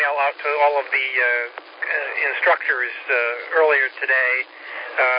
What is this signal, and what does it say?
Emergency Alert System (SAME) tones, heard as harsh buzzing bursts, with speech. It is heard from 0.9 to 4.7 s.